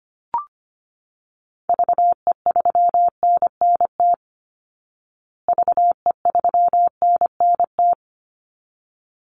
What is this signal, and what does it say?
Morse code, heard as a keyed tone, about 25 wpm, 700 Hz. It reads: VE3NNT VE3NNT
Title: Call Signs Repeated 3x - 25wpm